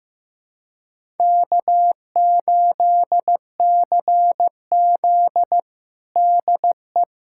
Morse code, heard as a keyed tone, about 16 wpm, 700 Hz. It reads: K8CZ DE
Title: USA Callsign DE Callsign - 4 Characters - 15wpm